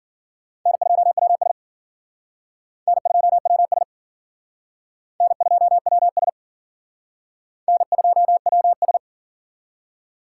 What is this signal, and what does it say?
Morse code, heard as a keyed tone, about 53 wpm, 700 Hz. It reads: N2WS N2WS N2WS N2WS